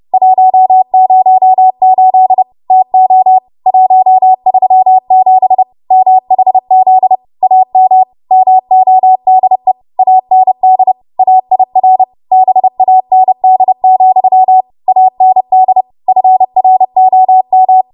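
Morse code, heard as a keyed tone, about 30 wpm, 750 Hz. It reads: 108 TO 137 MHZ AM MODE AND AIR BAND, AND FROM
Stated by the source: Morse practice transmission at 30 WPM.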